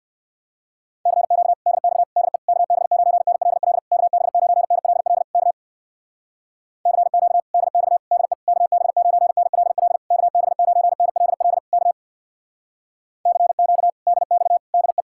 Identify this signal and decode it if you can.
Morse code, heard as a keyed tone, about 57 wpm, 700 Hz. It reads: CQ DX DE KB9NLL KB9NLL K CQ DX DE KB9NLL KB9NLL K CQ DX DE
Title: Speed-Racing Common QSO Phrases Encore - 50wpm